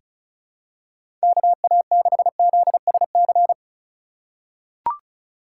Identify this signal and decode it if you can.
Morse code, heard as a keyed tone, about 36 wpm, 700 Hz. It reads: KA6ZSC
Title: USA Call Signs - 4x Character Spacing - 35wpm